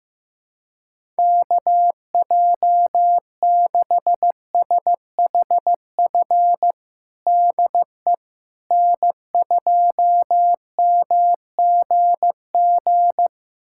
Morse code, heard as a keyed tone, about 15 wpm, 700 Hz. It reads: KJ6SHF DE N2MGG
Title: USA Callsign DE Callsign 15wpm